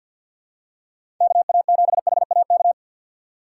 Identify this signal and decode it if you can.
Morse code, heard as a keyed tone, about 47 wpm, 700 Hz. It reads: KA7HAK
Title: USA Call Signs - 6 Characters - 4x Character Spacing - 50wpm